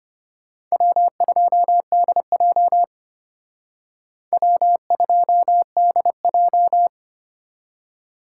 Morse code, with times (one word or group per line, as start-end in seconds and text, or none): 0.7-2.8 W2DJ
4.3-6.9 W2DJ